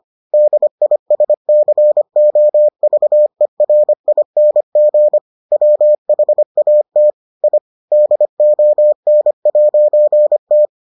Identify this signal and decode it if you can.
Morse code, heard as a keyed tone, about 25 wpm, 600 Hz. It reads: DISCOVERING WHAT I DON'T